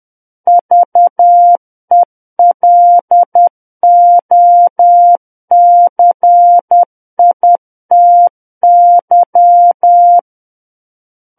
Morse code, heard as a keyed tone, about 10 wpm, 700 Hz. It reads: VELOCITY
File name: Book One / 01-Chapter 1 - 10 WPM